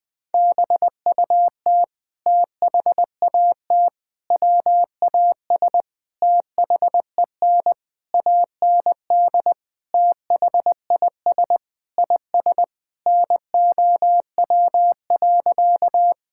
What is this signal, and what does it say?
Morse code, heard as a keyed tone, about 20 wpm, 700 Hz. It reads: BUT THAT WAS THEN AND THIS IS NOW.
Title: Sentences with 8-Word Phrases - Spoken Beginning - 20wpm